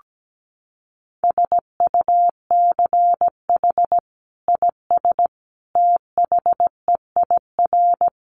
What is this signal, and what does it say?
Morse code, heard as a keyed tone, about 17 wpm, 700 Hz. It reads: SUCH IS THEIR
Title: Sentences from Top 900 Words 17wpm